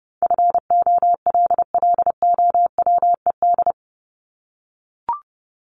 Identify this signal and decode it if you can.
Morse code, heard as a keyed tone, about 30 wpm, 700 Hz. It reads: FOLLOWED
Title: ED Suffix - Top 100 Words with Base Words 30wpm